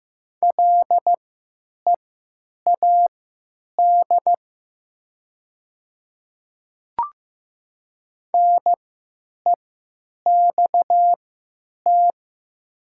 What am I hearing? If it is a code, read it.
Morse code, heard as a keyed tone, about 15 wpm, spaced as for 9 wpm, 700 Hz. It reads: LEAD NEXT
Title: 3x Character Spacing - Top 500 Words - 15wpm